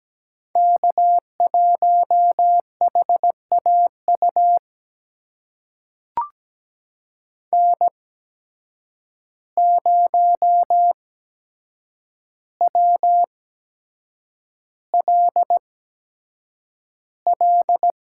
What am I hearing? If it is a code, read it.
Morse code, heard as a keyed tone, about 17 wpm, spaced as for 7 wpm, 700 Hz. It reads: K1HAU N0WLL
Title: USA Call Signs - 5 Characters - 8x Character Spacing - 17wpm